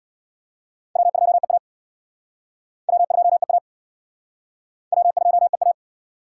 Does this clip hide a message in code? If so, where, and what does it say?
Yes: Morse code from 1.0 to 5.7 s, heard as a keyed tone, about 63 wpm, 700 Hz. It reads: W2EA W2EA W2EA